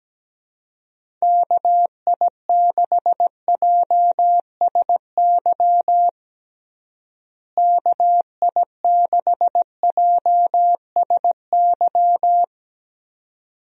Morse code, heard as a keyed tone, about 17 wpm, 700 Hz. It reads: KI6JSY KI6JSY
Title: USA Call Signs - 6 Characters - Repeated 2x - 17wpm